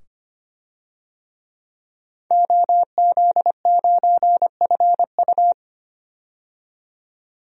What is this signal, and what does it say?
Morse code, heard as a keyed tone, about 25 wpm, 700 Hz. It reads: OZ9FU
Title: DXCC-100 Call Signs - 5 Characters - 25wpm